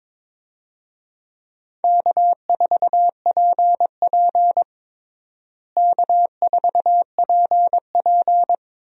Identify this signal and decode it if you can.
Morse code, heard as a keyed tone, about 23 wpm, 700 Hz. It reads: K4PP K4PP
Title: USA Call Signs - 4 Characters - Repeated 3x - 22wpm